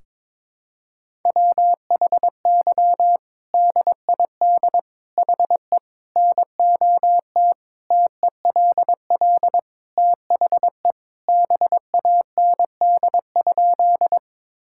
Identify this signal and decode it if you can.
Morse code, heard as a keyed tone, about 22 wpm, 700 Hz. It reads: WHY DID HE NOT TELL THE BAND?
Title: Sentences from Top 1600 Words 22wpm